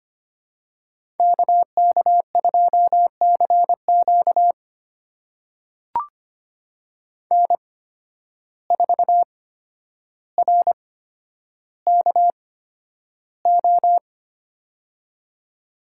Morse code, heard as a keyed tone, about 25 wpm, spaced as for 11 wpm, 700 Hz. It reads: KK2CQ N4RKO
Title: USA Call Signs - 5 Characters - 8x Character Spacing - 25wpm